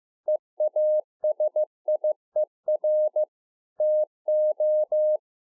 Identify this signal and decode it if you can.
Morse code, heard as a keyed tone, about 15 wpm, 610 Hz. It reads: EASIER TO
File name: 2011-04-mp3 / A SMALL TRANCEIVER 2 15WPM